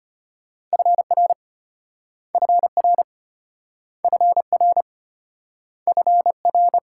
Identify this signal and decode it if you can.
Morse code, heard as a keyed tone, about 33 wpm, 700 Hz. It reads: FR FR FR FR